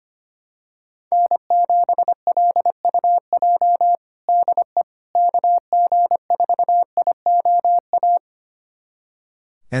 Morse code, heard as a keyed tone, about 25 wpm, 700 Hz. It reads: N7LUJ DE KG4IOA
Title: USA Callsign DE Callsign 25wpm